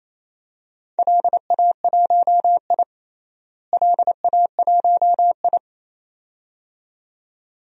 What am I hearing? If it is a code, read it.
Morse code, heard as a keyed tone, about 28 wpm, 700 Hz. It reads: LA1I LA1I
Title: DXCC-100 Call Signs - 4 Characters - Repeated 2x - 28wpm